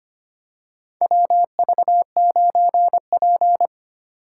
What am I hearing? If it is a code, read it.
Morse code, heard as a keyed tone, about 25 wpm, 700 Hz. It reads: WV9P
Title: USA Call Signs - 4 Characters - 3x Character Spacing - 25wpm